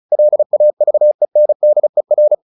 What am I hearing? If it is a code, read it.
Morse code, heard as a keyed tone, about 34 wpm, 600 Hz. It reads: LAVENDER
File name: the old man and the sea 2 35wpm